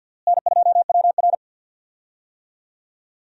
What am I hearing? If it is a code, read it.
Morse code, heard as a keyed tone, about 52 wpm, 700 Hz. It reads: N2WR
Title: USA Call Signs - 4 Characters - Repeated 3x - 50wpm